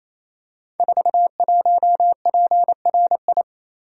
Morse code, heard as a keyed tone, about 27 wpm, 700 Hz. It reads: 41PRI
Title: Code Groups - 5 Letter:Numbers - 28wpm